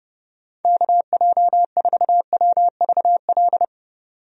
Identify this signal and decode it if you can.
Morse code, heard as a keyed tone, about 32 wpm, 700 Hz. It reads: KJ4WVL
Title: USA Call Signs - 6 Characters - 30wpm